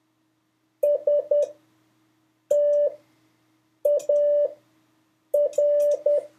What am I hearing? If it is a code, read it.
Morse code, heard as a keyed tone, about 10 wpm, spaced as for 6 wpm, 580 Hz. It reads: STAR
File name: 4of5-words-3